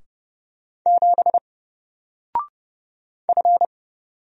Morse code, heard as a keyed tone, about 30 wpm, 700 Hz. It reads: 7 F
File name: Single-Letter-Number-30wpm